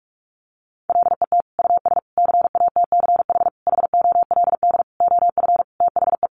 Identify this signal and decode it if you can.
Morse code, heard as a keyed tone, about 44 wpm, 700 Hz. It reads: LET US CATCH HOLD OF THE
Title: Sentences from Top 1500 Words 45wpm